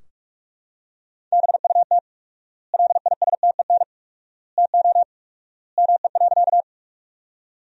Morse code, heard as a keyed tone, about 45 wpm, 700 Hz. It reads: BUT LISTEN TO ME.
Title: 4x Word-Spacing - Sentences from Top 400 Words - 45wpm